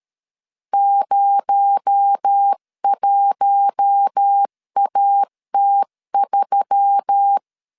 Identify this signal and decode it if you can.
Morse code, heard as a keyed tone, about 13 wpm, 800 Hz. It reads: 01AT3
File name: CW / DET / 1 6 1500C S C 12wpm 3533